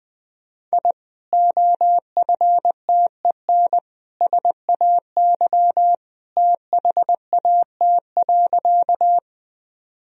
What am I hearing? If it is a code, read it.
Morse code, heard as a keyed tone, about 20 wpm, 700 Hz. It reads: I OFTEN SAY THAT.
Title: Sentences from Top 1000 Words - Review Part 1 - 20wpm